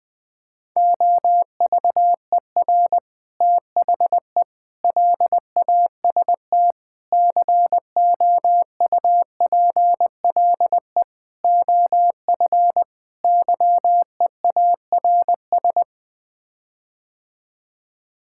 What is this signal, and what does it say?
Morse code, heard as a keyed tone, about 21 wpm, 700 Hz. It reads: OVER THE LAST COUPLE OF YEARS
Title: Sets of 6 Words - 20wpm